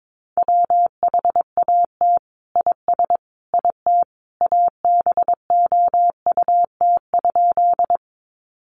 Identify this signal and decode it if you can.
Morse code, heard as a keyed tone, about 22 wpm, 700 Hz. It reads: WHAT IS IT ABOUT?
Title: Sentences from Top 100 Words 22wpm